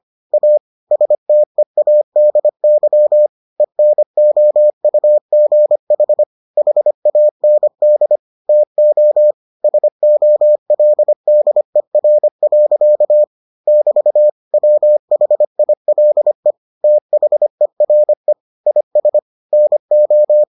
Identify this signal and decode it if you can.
Morse code, heard as a keyed tone, about 25 wpm, 600 Hz. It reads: A STEADY ENOUGH HAND TO SOLDER. <BT> WHILE THERE IS NO